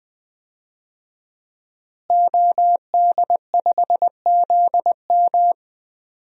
Morse code, heard as a keyed tone, about 20 wpm, 700 Hz. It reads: OD5ZM
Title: DXCC-100 Call Signs - 5 Characters - Repeated 3x - 20wpm